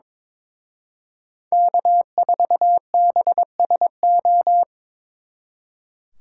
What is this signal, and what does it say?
Morse code, heard as a keyed tone, about 26 wpm, 700 Hz. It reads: K4BSO
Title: Speed Racing - USA 1x3 Call Signs 22wpm